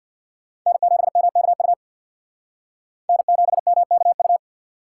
Morse code, heard as a keyed tone, about 64 wpm, 700 Hz. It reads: N7MKU N7MKU